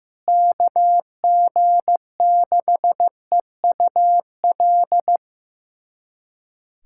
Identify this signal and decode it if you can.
Morse code, heard as a keyed tone, about 15 wpm, 700 Hz. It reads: KG6EUL
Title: USA Callsign DE Callsign - 6 Characters - 15wpm